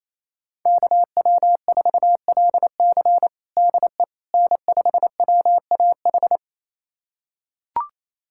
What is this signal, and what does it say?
Morse code, heard as a keyed tone, about 28 wpm, 700 Hz. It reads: KW4LC DE N5WAH
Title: USA Callsign DE Callsign - 5 Characters - 28wpm